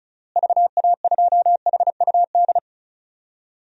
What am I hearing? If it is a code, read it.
Morse code, heard as a keyed tone, about 36 wpm, 700 Hz. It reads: VA2HUD
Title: DXCC-100 Call Signs - 6 Characters - Repeated 2x - 35wpm